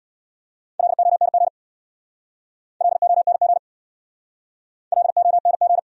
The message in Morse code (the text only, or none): LONG LONG LONG